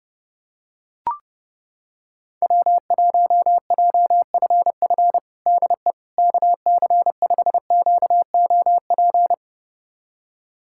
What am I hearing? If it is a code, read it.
Morse code, heard as a keyed tone, about 32 wpm, 700 Hz. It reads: W1JFF DE KC5QOP
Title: USA Callsign DE Callsign 30wpm